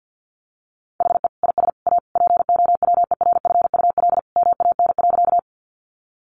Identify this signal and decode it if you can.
Morse code, heard as a keyed tone, about 50 wpm, 700 Hz. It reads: HE IS A POWERFUL MAN.
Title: Sentences from Top 1300 Words 50wpm